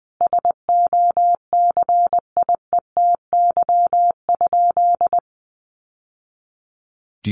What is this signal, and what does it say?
Morse code, heard as a keyed tone, about 20 wpm, 700 Hz. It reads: SOCIETY?
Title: Sentences from Top 700 Words 20wpm